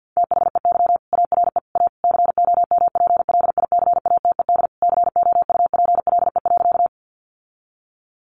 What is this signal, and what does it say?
Morse code, heard as a keyed tone, about 48 wpm, 700 Hz. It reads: THEY ARE A COMPLICATED COUPLE.
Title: Sentences from Top 1200 Words 50wpm